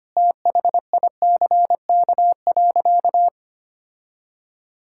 Morse code, heard as a keyed tone, about 25 wpm, 700 Hz. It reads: THICK.
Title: Sentences from Top 1600 Words 25wpm